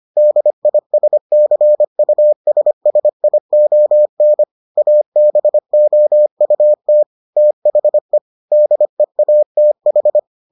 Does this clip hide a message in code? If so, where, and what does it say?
Yes: Morse code from 0.2 to 10.2 s, heard as a keyed tone, about 25 wpm, 600 Hz. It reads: DISCUSSION ABOUT THE DEATH